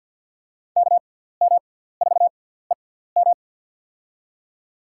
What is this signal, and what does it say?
Morse code, heard as a keyed tone, about 48 wpm, spaced as for 17 wpm, 700 Hz. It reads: KM4EM
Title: USA Call Signs - 5 Characters - 6x Character Spacing - 50wpm